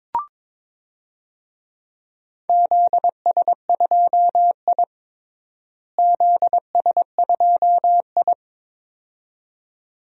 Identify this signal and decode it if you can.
Morse code, heard as a keyed tone, about 22 wpm, 700 Hz. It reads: ZS2I ZS2I
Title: DXCC-100 Call Signs - 4 Characters - Repeated 2x - 22wpm